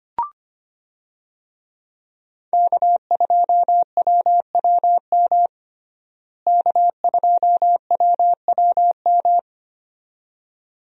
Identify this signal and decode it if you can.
Morse code, heard as a keyed tone, about 25 wpm, 700 Hz. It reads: K2WWM K2WWM
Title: USA Call Signs - 5 Characters - Repeated 2x - 25wpm